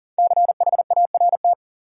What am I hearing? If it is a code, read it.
Morse code, heard as a keyed tone, about 40 wpm, 700 Hz. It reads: CHART
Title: Sentences from Top 1300 Words - New Word Drill Alt - 40wpm